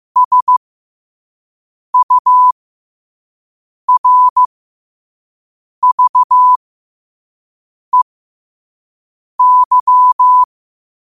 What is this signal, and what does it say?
Morse code, heard as a keyed tone, about 15 wpm, spaced as for 5 wpm, 1000 Hz. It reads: SURVEY
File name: qotdfiles / 5wpm